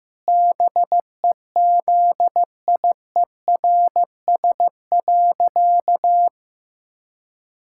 Morse code, heard as a keyed tone, about 15 wpm, 700 Hz. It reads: BEZIERS.
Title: Pangram Challenge v2 - 15wpm